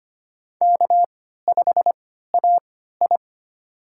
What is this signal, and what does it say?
Morse code, heard as a keyed tone, about 25 wpm, spaced as for 14 wpm, 700 Hz. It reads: K5AI
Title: USA Call Signs - 4 Characters - 3x Character Spacing - 25wpm